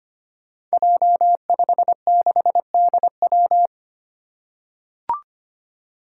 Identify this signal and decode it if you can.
Morse code, heard as a keyed tone, about 25 wpm, 700 Hz. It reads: J56DW
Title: Five Challenging Letter-Number Combinations 25wpm